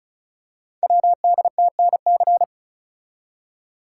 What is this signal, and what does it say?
Morse code, heard as a keyed tone, about 36 wpm, 690 Hz. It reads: WDTNC
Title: Code Groups - 5 Letters - 35wpm